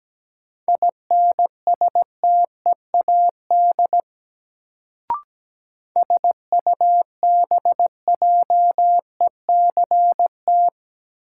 Morse code, heard as a keyed tone, about 17 wpm, 700 Hz. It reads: INSTEAD SUBJECT
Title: Sentences from Top 800 Words - New Word Drill - 17wpm